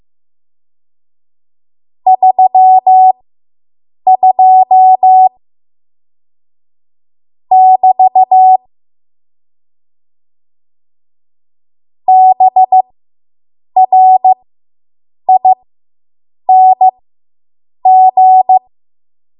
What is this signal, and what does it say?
Morse code, heard as a keyed tone, about 15 wpm, spaced as for 7 wpm, 750 Hz. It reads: 32 <BT> BRING